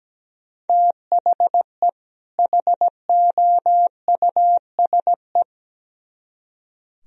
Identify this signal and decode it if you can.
Morse code, heard as a keyed tone, about 19 wpm, 700 Hz. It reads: THE HOUSE
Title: Speed-Racing - Sets of 2 Words 17wpm